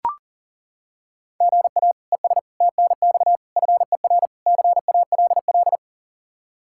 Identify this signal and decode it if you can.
Morse code, heard as a keyed tone, about 40 wpm, 700 Hz. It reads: GA ES TNX FER CALL
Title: Practice QSO - Exchange 1 - Beginning - 40wpm